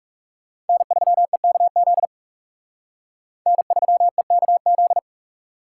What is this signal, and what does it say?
Morse code, heard as a keyed tone, about 36 wpm, 700 Hz. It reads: N3EKZ N3EKZ
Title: Speed Racing - USA 1x3 Call Signs 30wpm